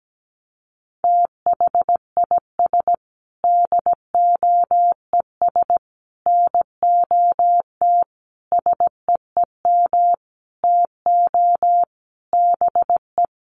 Morse code, heard as a keyed tone, about 17 wpm, 700 Hz. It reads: THIS DOES NOT SEEM TO BE
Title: Sentences from Top 1400 Words 17wpm